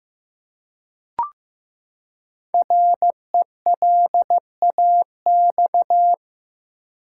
Morse code, heard as a keyed tone, about 15 wpm, 700 Hz. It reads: RELAX
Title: Five Letter Words - 15wpm